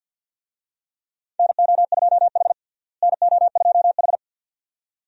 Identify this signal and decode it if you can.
Morse code, heard as a keyed tone, about 47 wpm, 700 Hz. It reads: NO2H NO2H
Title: USA Call Signs - 4 Characters - Repeated 2x - 50wpm